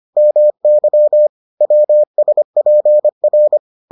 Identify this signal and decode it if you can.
Morse code, heard as a keyed tone, about 25 wpm, 600 Hz. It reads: MY WSPR